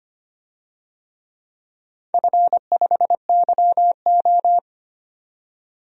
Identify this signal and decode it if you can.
Morse code, heard as a keyed tone, about 25 wpm, 700 Hz. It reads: F5YO
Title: DXCC-100 Call Signs - 4 Characters - 25wpm